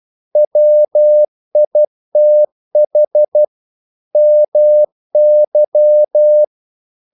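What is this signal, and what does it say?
Morse code, heard as a keyed tone, about 12 wpm, 600 Hz. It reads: WITH MY
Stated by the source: Morse practice sending English quotations.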